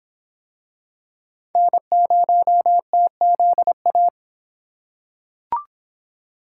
Morse code, heard as a keyed tone, about 23 wpm, 700 Hz. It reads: N0TZA